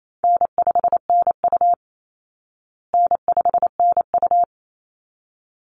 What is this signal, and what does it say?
Morse code, heard as a keyed tone, about 28 wpm, 700 Hz. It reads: N5NU N5NU
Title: Call Signs Repeated 3x - 28wpm